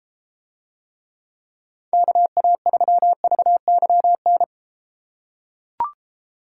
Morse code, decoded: KA3VYN

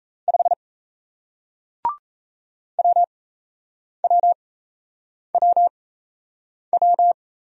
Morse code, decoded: 5 WWWW

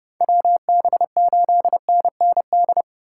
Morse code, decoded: WB8NND

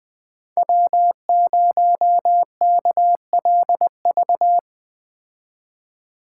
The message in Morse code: W0KLV

